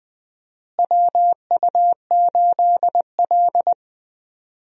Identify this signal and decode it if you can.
Morse code, heard as a keyed tone, about 21 wpm, 700 Hz. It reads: WU8L
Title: USA Call Signs - 4 Characters - Repeated 3x - 20wpm